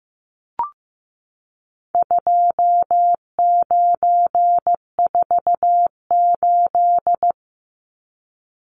Morse code, heard as a keyed tone, about 15 wpm, 700 Hz. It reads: 2948
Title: Numbers - 4-Digit 15wpm